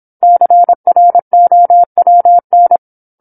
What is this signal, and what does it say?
Morse code, heard as a keyed tone, about 27 wpm, 700 Hz. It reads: CROWN
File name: 08-Chapter-08 26 WPM